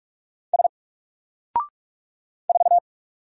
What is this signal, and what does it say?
Morse code, heard as a keyed tone, about 44 wpm, 700 Hz. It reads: S 4
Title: Single Letter-Number Alt 45wpm